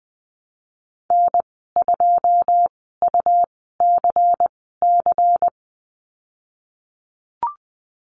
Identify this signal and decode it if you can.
Morse code, heard as a keyed tone, about 20 wpm, spaced as for 15 wpm, 700 Hz. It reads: N2UCC
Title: 122 Lesson 20g - Number 3 - Call Signs 20wpm